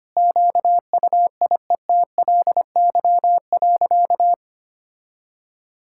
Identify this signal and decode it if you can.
Morse code, heard as a keyed tone, about 25 wpm, 700 Hz. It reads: QUIETLY.
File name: practice / 25wpm / Sentences with 2-Word Phrases - 25wpm